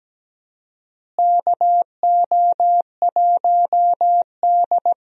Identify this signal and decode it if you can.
Morse code, heard as a keyed tone, about 17 wpm, 700 Hz. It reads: KO1D
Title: USA Call Signs - 4 Characters - Repeated 3x - 17wpm